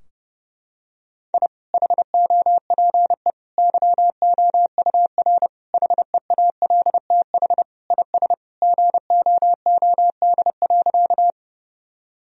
Morse code, heard as a keyed tone, about 30 wpm, 700 Hz. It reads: I HOPE YOUR HEALTH IS GOOD.
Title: Sentences from Top 1000 Words - Review Part 3 - 30wpm